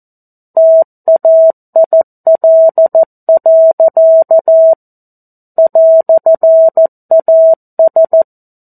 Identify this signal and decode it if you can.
Morse code, heard as a keyed tone, about 14 wpm, 650 Hz. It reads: TAIL. "AS
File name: 02-Chapter-02 14 WPM